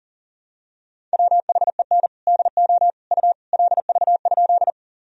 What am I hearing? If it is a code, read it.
Morse code, heard as a keyed tone, about 37 wpm, 700 Hz. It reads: WHEN DO U LV?
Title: Sentences from Top 400 Words and Abbreviations 40wpm